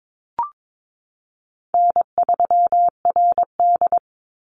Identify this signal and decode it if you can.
Morse code, heard as a keyed tone, about 22 wpm, 700 Hz. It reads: N3RD
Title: Call Signs - 22wpm